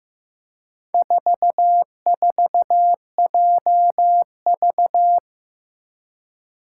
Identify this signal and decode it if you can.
Morse code, heard as a keyed tone, about 15 wpm, 700 Hz. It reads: 44JV
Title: Four Challenging Letter-Number Combinations 15wpm